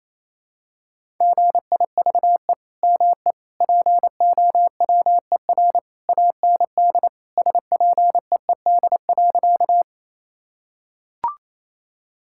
Morse code, heard as a keyed tone, about 28 wpm, 700 Hz. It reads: GIVE ME POWER AND SPEED.
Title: Sentences from Top 1000 Words - Review Part 5 - 28wpm